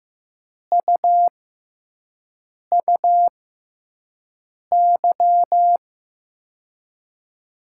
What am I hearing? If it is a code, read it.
Morse code, heard as a keyed tone, about 15 wpm, spaced as for 5 wpm, 700 Hz. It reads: UUY
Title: USA Call Signs - 6 Characters - 6x Character Spacing - 15wpm